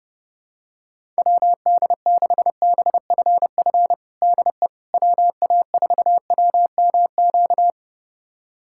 Morse code, heard as a keyed tone, about 30 wpm, 700 Hz. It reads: WD6BFF DE WA4WMQ
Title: USA Callsign DE Callsign - 6 Characters - 30wpm